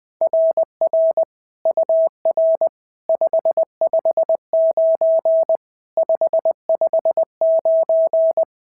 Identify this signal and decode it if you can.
Morse code, heard as a keyed tone, about 19 wpm, 690 Hz. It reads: RR UR 559 559
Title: SOTA QSO Practice 20wpm